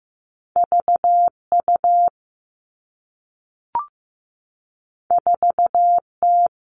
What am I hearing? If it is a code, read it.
Morse code, heard as a keyed tone, about 15 wpm, 700 Hz. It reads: V U 4 T
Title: Two Challenging Letter-Number Combinations 15wpm